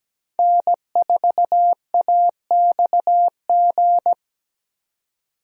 Morse code, heard as a keyed tone, about 17 wpm, 700 Hz. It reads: N4AXG
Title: USA Call Signs - 5 Characters - 17wpm